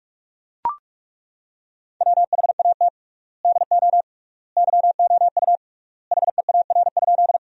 Morse code, heard as a keyed tone, about 41 wpm, 700 Hz. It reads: WHAT DO YOU HEAR?
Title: 3x Word-Spacing - Sentences from Top 300 Words - 45wpm